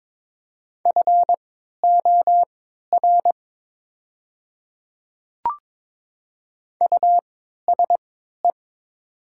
Morse code, heard as a keyed tone, about 22 wpm, spaced as for 13 wpm, 700 Hz. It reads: FOR USE
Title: 3x Character Spacing - Top 100 Words - 22wpm